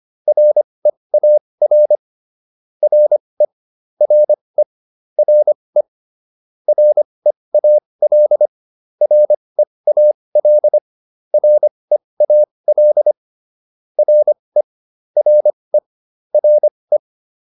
Morse code, heard as a keyed tone, about 25 wpm, spaced as for 20 wpm, 600 Hz. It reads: REAR RE RE RE REAL REAL REAL RE RE RE